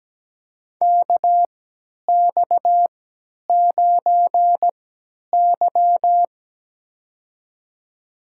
Morse code, heard as a keyed tone, about 17 wpm, spaced as for 10 wpm, 700 Hz. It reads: KX9Y
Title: USA Call Signs - 4 Characters - 3x Character Spacing - 17wpm